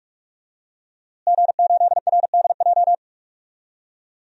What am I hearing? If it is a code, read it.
Morse code, heard as a keyed tone, about 41 wpm, 700 Hz. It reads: G8RDJ
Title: DXCC-100 Call Signs - 5 Characters - Repeated 2x - 45wpm